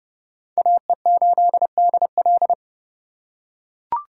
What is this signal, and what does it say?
Morse code, heard as a keyed tone, about 30 wpm, 700 Hz. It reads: AE8DL
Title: USA Call Signs - 5 Characters - 2x Character Spacing - 30wpm